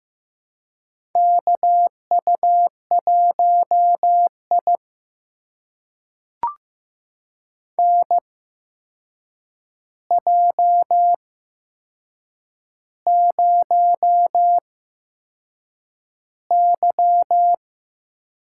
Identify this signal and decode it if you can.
Morse code, heard as a keyed tone, about 15 wpm, spaced as for 6 wpm, 700 Hz. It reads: KU1I NJ0Y
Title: USA Call Signs - 4 Characters - 8x Character Spacing - 15wpm